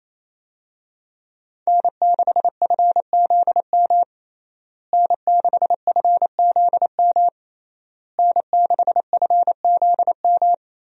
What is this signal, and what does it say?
Morse code, heard as a keyed tone, about 28 wpm, 700 Hz. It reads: N6FZM N6FZM N6FZM
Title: USA Call Signs - 5 Characters - Repeated 3x - 28wpm